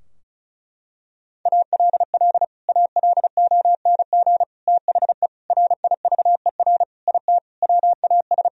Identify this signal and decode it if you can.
Morse code, heard as a keyed tone, about 36 wpm, 700 Hz. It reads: ALL ALONG THE RIVER IT WAS
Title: Sentences from Top 1000 Words - Review Part 2 - 35wpm